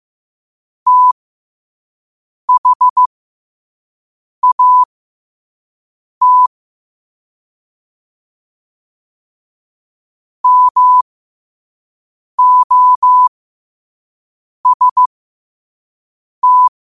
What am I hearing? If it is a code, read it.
Morse code, heard as a keyed tone, about 15 wpm, spaced as for 5 wpm, 1000 Hz. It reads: THAT MOST
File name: qotdfiles / 5wpm